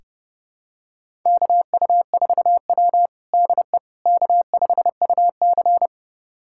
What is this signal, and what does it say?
Morse code, heard as a keyed tone, about 30 wpm, 700 Hz. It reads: KU4W DE K5UC
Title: USA Callsign DE Callsign - 4 Characters - 30wpm